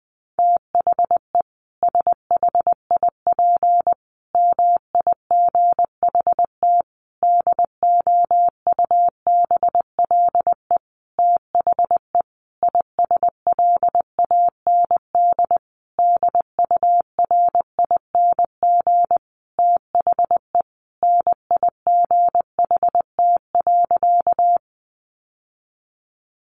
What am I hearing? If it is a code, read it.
Morse code, heard as a keyed tone, about 20 wpm, 700 Hz. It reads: THE SHIP MIGHT DOUBLE THE ISLAND DURING THE NIGHT.
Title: Sentences from Top 1400 Words 20wpm